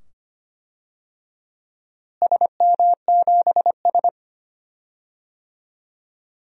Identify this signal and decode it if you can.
Morse code, heard as a keyed tone, about 25 wpm, 700 Hz. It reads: SM7S
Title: DXCC-100 Call Signs - 4 Characters - Repeated 2x - 25wpm